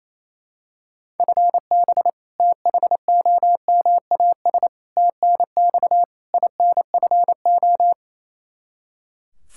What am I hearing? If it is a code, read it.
Morse code, heard as a keyed tone, about 27 wpm, 700 Hz. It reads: FB THOMAS TNX INFO
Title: Practice QSO - Exchange 3 - More Info - 28wpm